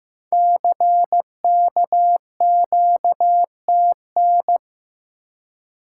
Morse code, heard as a keyed tone, about 15 wpm, 700 Hz. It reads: CKQTN
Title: Code Groups - 5 Letters - 15wpm